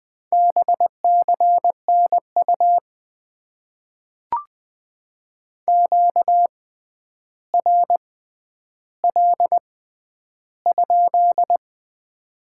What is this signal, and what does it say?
Morse code, heard as a keyed tone, about 20 wpm, spaced as for 10 wpm, 700 Hz. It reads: BCNU QRL?
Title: Top 90 QSO Elements - 6x Character Spacing - 20wpm